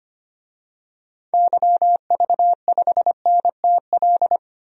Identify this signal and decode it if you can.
Morse code, heard as a keyed tone, about 25 wpm, 700 Hz. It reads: YV5NTL
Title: DXCC-100 Call Signs - 6 Characters - 25wpm